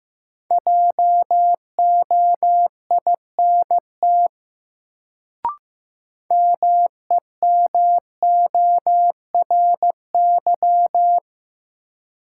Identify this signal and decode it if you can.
Morse code, heard as a keyed tone, about 15 wpm, 700 Hz. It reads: JOINT MEMORY
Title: Sentences from Top 1300 Words - New Word Drill Alt - 15wpm